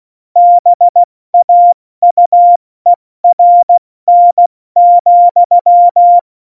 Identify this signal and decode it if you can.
Morse code, heard as a keyed tone, about 16 wpm, 700 Hz. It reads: BAUERN,
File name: Morsekurs / Witze / Witz 21 Juli 2020, 70 F80